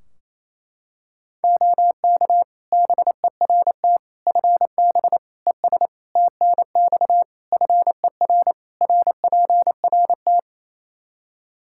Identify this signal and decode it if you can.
Morse code, heard as a keyed tone, about 28 wpm, 700 Hz. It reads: OK BERT FB ES TNX FER RPRT
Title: Practice QSO - Exchange 2 - Info - 28wpm